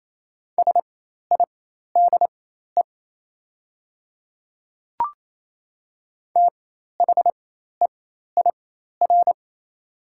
Morse code, heard as a keyed tone, about 28 wpm, spaced as for 13 wpm, 700 Hz. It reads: SIDE THEIR
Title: 4x Character Spacing - Top 100 Words - 28wpm